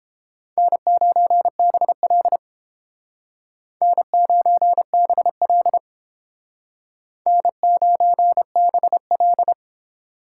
Morse code, heard as a keyed tone, about 30 wpm, 700 Hz. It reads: N9BL N9BL N9BL